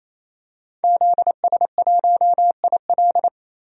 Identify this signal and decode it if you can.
Morse code, heard as a keyed tone, about 27 wpm, 690 Hz. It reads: ZS1IL
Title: DXCC-100 Call Signs - 5 Characters - 28wpm